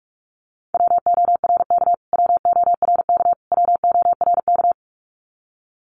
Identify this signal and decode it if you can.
Morse code, heard as a keyed tone, about 44 wpm, 700 Hz. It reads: WORK WORK WORK
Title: Top 500 Words Repeated 45wpm